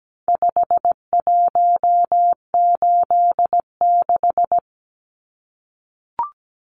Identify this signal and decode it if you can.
Morse code, heard as a keyed tone, about 17 wpm, 700 Hz. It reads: 5186
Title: Numbers - 4-Digit 17wpm